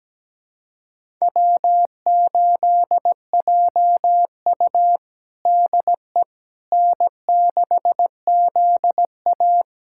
Morse code, heard as a keyed tone, about 17 wpm, 700 Hz. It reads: W8JU DE N6ZA
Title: USA Callsign DE Callsign - 4 Characters - 17wpm